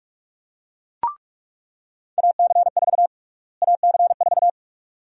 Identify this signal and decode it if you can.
Morse code, heard as a keyed tone, about 42 wpm, 700 Hz. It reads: AC4 AC4
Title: USA Call Sign Prefixes - Repeated 2x - 45wpm